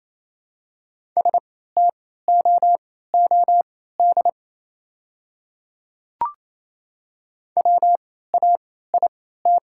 Morse code, heard as a keyed tone, about 28 wpm, spaced as for 16 wpm, 690 Hz. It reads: STOOD WAIT
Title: 3x Character Spacing - Top 500 Words - 28wpm